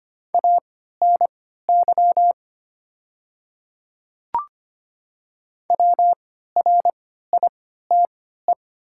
Morse code, heard as a keyed tone, about 25 wpm, spaced as for 14 wpm, 700 Hz. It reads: ANY WRITE
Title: 3x Character Spacing - Top 300 Words - 25wpm